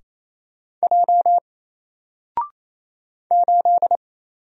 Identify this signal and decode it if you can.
Morse code, heard as a keyed tone, about 27 wpm, 700 Hz. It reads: J 8
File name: Single-Letter-Number-28wpm